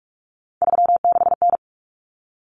Morse code, heard as a keyed tone, about 44 wpm, 700 Hz. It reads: 36N
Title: Common Cut Numbers - 3-Digit 45wpm